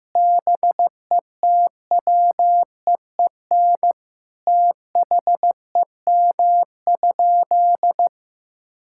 Morse code, heard as a keyed tone, about 15 wpm, 700 Hz. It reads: BETWEEN THEM?
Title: Sentences from Top 1000 Words - Review Part 6 - 15wpm